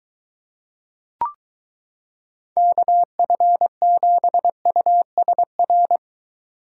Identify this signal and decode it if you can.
Morse code, heard as a keyed tone, about 23 wpm, 700 Hz. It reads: KF7USR